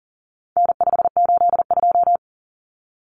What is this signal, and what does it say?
Morse code, heard as a keyed tone, about 40 wpm, 700 Hz. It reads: N582
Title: Common Cut Numbers - 4-Digit 40wpm